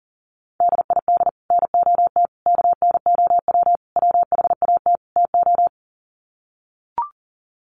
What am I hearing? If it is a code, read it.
Morse code, heard as a keyed tone, about 40 wpm, 700 Hz. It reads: DID NOT KNOW WHAT TO
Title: Sets of 5 Words 40wpm